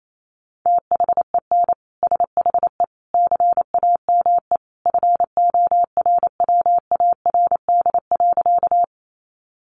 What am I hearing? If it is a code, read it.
Morse code, heard as a keyed tone, about 28 wpm, 700 Hz. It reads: THEN SHE CAME FORWARD.
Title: Sentences from Top 1300 Words 28wpm